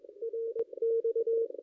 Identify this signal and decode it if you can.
Morse code, heard as a keyed tone, about 22 wpm, 460 Hz. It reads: RX
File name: contests / ss-cw-k5tr-op-n5rz -0400